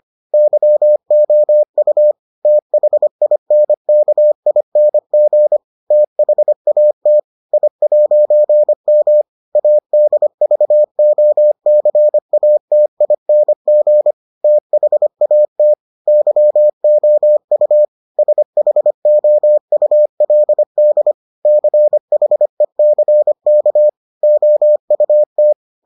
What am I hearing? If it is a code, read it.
Morse code, heard as a keyed tone, about 25 wpm, 600 Hz. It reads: YOU THINKING THAT I'M ADVOCATING THAT YOU SHOULD CHECK OUT